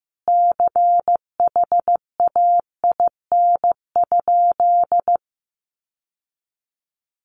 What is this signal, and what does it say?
Morse code, heard as a keyed tone, about 15 wpm, 700 Hz. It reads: CHAIN?
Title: Sentences from Top 1500 Words 15wpm